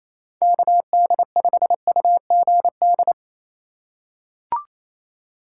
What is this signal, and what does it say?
Morse code, heard as a keyed tone, about 28 wpm, 700 Hz. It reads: KD5UGD
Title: USA Call Signs - 6 Characters - 2x Character Spacing - 28wpm